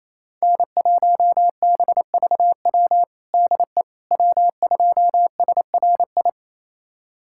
Morse code, heard as a keyed tone, about 28 wpm, 700 Hz. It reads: N1BVW DE W2SRI
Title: USA Callsign DE Callsign - 5 Characters - 28wpm